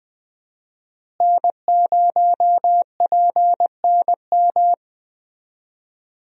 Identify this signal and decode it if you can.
Morse code, heard as a keyed tone, about 20 wpm, 700 Hz. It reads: N0PNM
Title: USA Call Signs - 5 Characters - Repeated 2x - 20wpm